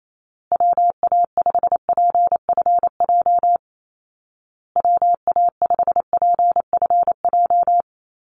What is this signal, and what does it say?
Morse code, heard as a keyed tone, about 28 wpm, 700 Hz. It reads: WA5PFJ WA5PFJ
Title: Call Signs Repeated 2x - 28wpm